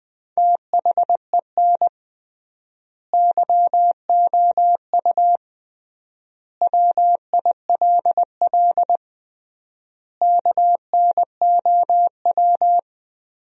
Morse code, heard as a keyed tone, about 20 wpm, 700 Hz. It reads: THEN YOU WILL KNOW
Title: 3x Word-Spacing - Sentences from Top 400 Words - 20wpm